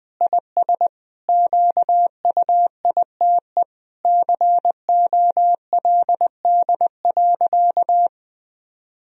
Morse code, heard as a keyed tone, about 21 wpm, 700 Hz. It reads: IS QUITE COLD.
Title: Sentences from Top 1000 Words - Review Part 4 - 20wpm